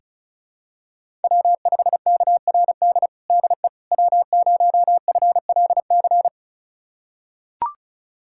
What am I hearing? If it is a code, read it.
Morse code, heard as a keyed tone, about 36 wpm, 700 Hz. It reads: W5KRD DE W0FLC
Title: USA Callsign DE Callsign - 5 Characters - 35wpm